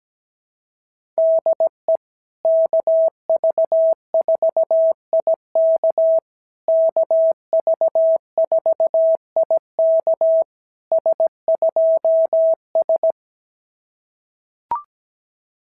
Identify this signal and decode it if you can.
Morse code, heard as a keyed tone, about 17 wpm, 650 Hz. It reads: DE KV4IK KV4IK S2S
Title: SOTA QSO Practice 17wpm